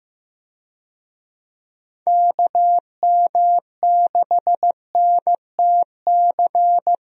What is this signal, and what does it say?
Morse code, heard as a keyed tone, about 15 wpm, 700 Hz. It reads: KM6NTC
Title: USA Call Signs - 6 Characters - Repeated 3x - 15wpm